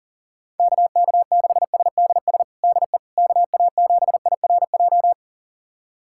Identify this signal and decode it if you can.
Morse code, heard as a keyed tone, about 40 wpm, 700 Hz. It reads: KK6SDS DE KA7IRJ
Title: USA Callsign DE Callsign - 6 Characters - 40wpm